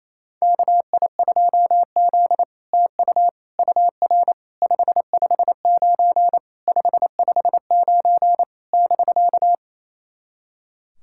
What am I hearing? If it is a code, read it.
Morse code, heard as a keyed tone, about 29 wpm, 690 Hz. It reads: KI2Z TU UR 559 559 <BK>
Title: SOTA QSO Practice 28wpm